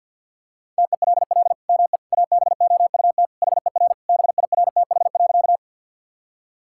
Morse code, heard as a keyed tone, about 50 wpm, 700 Hz. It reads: TELL ME ABOUT HER BIRTH.
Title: Sentences from Top 1000 Words - Review Part 5 - 50wpm